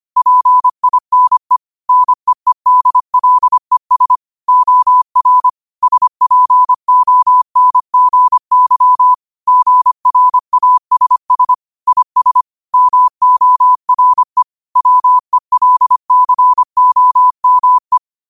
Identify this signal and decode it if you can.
Morse code, heard as a keyed tone, about 25 wpm, 1000 Hz. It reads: PINE NEEDLES OR SPONGY GRASS IS MORE WELCOME
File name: qotdfiles / 25wpm